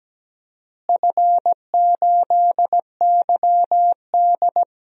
Morse code, heard as a keyed tone, about 17 wpm, 700 Hz. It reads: F8YD
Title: DXCC-100 Call Signs - 4 Characters - Repeated 2x - 17wpm